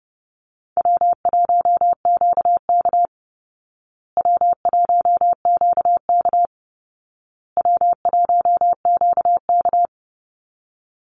Morse code, heard as a keyed tone, about 30 wpm, 700 Hz. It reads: W1QK W1QK W1QK
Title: Easy Call Signs Repeated 3x - 30wpm